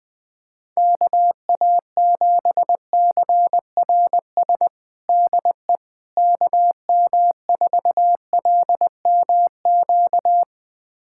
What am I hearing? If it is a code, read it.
Morse code, heard as a keyed tone, about 21 wpm, 700 Hz. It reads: KA7CRS DE KM4LMQ
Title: USA Callsign DE Callsign 20wpm